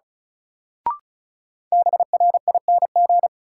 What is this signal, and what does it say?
Morse code, heard as a keyed tone, about 36 wpm, 700 Hz. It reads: BRING
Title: Sets of 5 Words Encore - New Word Drill - 35wpm